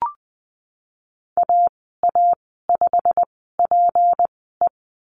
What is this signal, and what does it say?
Morse code, heard as a keyed tone, about 20 wpm, spaced as for 15 wpm, 700 Hz. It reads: AA5PE
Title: 231 Lesson 37g - Letter x - Call Signs 20wpm